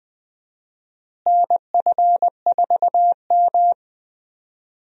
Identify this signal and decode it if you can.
Morse code, heard as a keyed tone, about 20 wpm, 700 Hz. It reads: NF4M
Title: USA Call Signs - 4 Characters - Repeated 2x - 20wpm